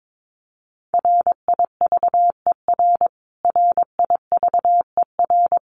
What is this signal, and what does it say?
Morse code, heard as a keyed tone, about 22 wpm, 700 Hz. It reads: RIVER RIVER
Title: Top 300 Words Repeated 22wpm